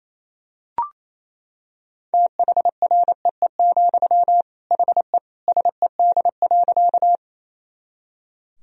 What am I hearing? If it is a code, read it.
Morse code, heard as a keyed tone, about 28 wpm, 700 Hz. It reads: THREE, HE SED.
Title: Sentences from Top 200 Words and Abbreviations 28wpm